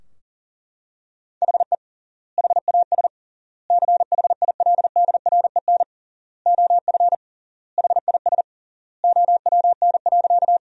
Morse code, heard as a keyed tone, about 37 wpm, 700 Hz. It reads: HE HAS CHILDREN OF HIS OWN.
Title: 3x Word-Spacing - Sentences from Top 300 Words - 40wpm